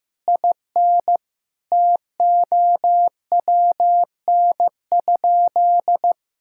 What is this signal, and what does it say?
Morse code, heard as a keyed tone, about 15 wpm, 700 Hz. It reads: IN TOWN?
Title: Sentences from Top 1000 Words - Review Part 6 - 15wpm